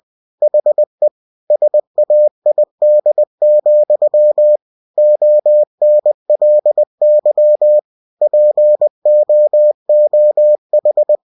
Morse code, heard as a keyed tone, about 20 wpm, 600 Hz. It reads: HE SAID, ONLY POOH
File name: cw-books / winnie-the-pooh / winnie-20wpm-page-0011